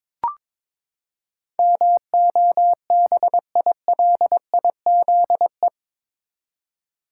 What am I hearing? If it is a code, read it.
Morse code, heard as a keyed tone, about 22 wpm, 700 Hz. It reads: MOBILIZE
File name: practice / 22wpm / Eight Letter Words - 22wpm